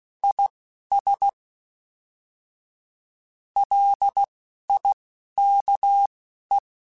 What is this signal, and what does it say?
Morse code, heard as a keyed tone, about 16 wpm, spaced as for 12 wpm, 800 Hz. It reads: IS LIKE
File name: Weather Sets / Weather Sets-mp3s / Weather 4 17 @10